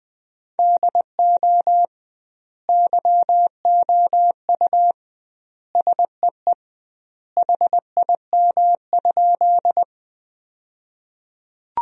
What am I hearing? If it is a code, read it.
Morse code, heard as a keyed tone, about 21 wpm, 700 Hz. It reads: DO YOU SEE HIM?
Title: 2x Word-Spacing - Sentences from Top 100 Words - 20wpm